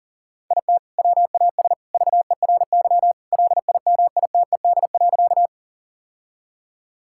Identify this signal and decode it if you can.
Morse code, heard as a keyed tone, about 40 wpm, 700 Hz. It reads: IT WAS VERY LIMITED.
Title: Sentences from Top 1000 Words - Review Part 6 - 40wpm